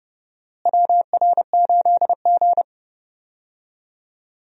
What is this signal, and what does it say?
Morse code, heard as a keyed tone, about 31 wpm, 700 Hz. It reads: WR8G
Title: USA Call Signs - 4 Characters - Repeated 2x - 30wpm